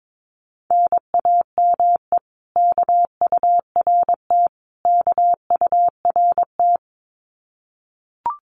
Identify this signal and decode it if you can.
Morse code, heard as a keyed tone, about 22 wpm, 700 Hz. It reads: NAME KURT KURT
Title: Common QSO Phrases Encore 22wpm